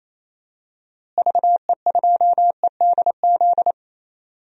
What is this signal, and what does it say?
Morse code, heard as a keyed tone, about 28 wpm, 700 Hz. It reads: VE2EDZ
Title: DXCC-100 Call Signs - 6 Characters - Repeated 3x - 28wpm